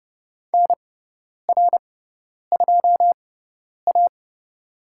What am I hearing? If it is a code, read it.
Morse code, heard as a keyed tone, about 30 wpm, spaced as for 10 wpm, 700 Hz. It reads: NR2A
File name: Lesson 18g - Letter m - Call Signs 30wpm